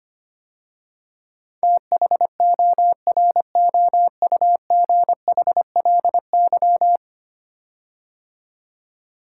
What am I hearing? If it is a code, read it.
Morse code, heard as a keyed tone, about 25 wpm, 700 Hz. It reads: THOROUGHLY
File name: 10 Letter Words - First 3 Letters Given - 25wpm